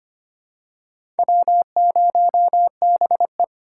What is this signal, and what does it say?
Morse code, heard as a keyed tone, about 25 wpm, 700 Hz. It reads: W0BE
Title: USA Call Signs - 2x Character Spacing - 25wpm